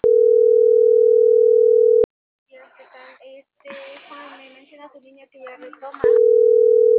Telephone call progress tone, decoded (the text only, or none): ringback tone